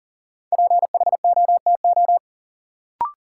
Morse code, heard as a keyed tone, about 40 wpm, 700 Hz. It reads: PHOTO
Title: Sentences from Top 1200 Words - New Word Drill - 40wpm